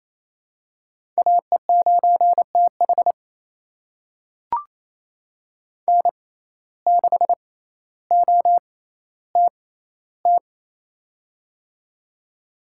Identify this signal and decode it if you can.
Morse code, heard as a keyed tone, about 28 wpm, spaced as for 14 wpm, 700 Hz. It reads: AE9TH N6OTT
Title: USA Call Signs - 5 Characters - 6x Character Spacing - 28wpm